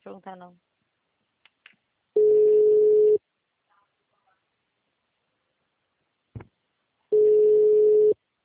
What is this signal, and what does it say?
Ringback tone (Europe, 425 Hz), with speech.